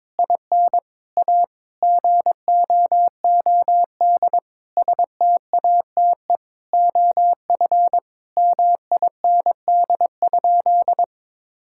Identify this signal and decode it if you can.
Morse code, heard as a keyed tone, about 21 wpm, 700 Hz. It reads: IN A GOOD STATE OF MIND?
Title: Sentences from Top 1000 Words - Review Part 5 - 22wpm